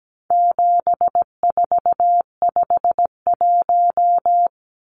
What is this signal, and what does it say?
Morse code, heard as a keyed tone, about 17 wpm, 700 Hz. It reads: 7451
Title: Numbers - 4-Digit 17wpm